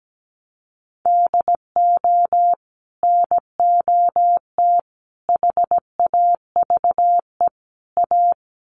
Morse code, heard as a keyed tone, about 17 wpm, 700 Hz. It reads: DO NOT HAVE A
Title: Sets of 4 Words 17wpm